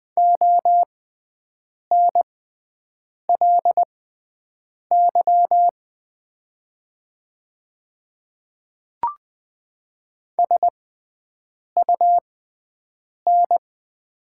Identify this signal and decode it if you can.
Morse code, heard as a keyed tone, about 20 wpm, spaced as for 7 wpm, 700 Hz. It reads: ONLY SUN
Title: 6x Character Spacing - Top 500 Words - 20wpm